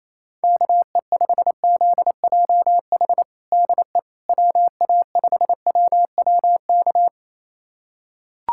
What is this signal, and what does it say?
Morse code, heard as a keyed tone, about 28 wpm, 700 Hz. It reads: KE5ZJH DE WA5WWK
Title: USA Callsign DE Callsign - 6 Characters - 28wpm